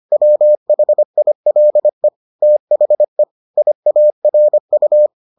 Morse code, heard as a keyed tone, about 25 wpm, 600 Hz. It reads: WHILE THE IARU